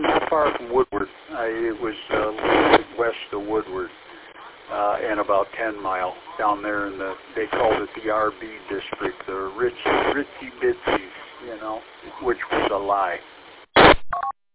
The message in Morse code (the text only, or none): none